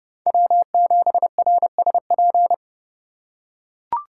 Morse code, heard as a keyed tone, about 28 wpm, 700 Hz. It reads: W7RSP